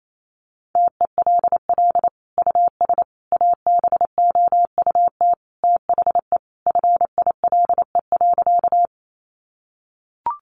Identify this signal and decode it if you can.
Morse code, heard as a keyed tone, about 28 wpm, 700 Hz. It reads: TELL US ABOUT THE FILE.
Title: Sentences from Top 1200 Words 28wpm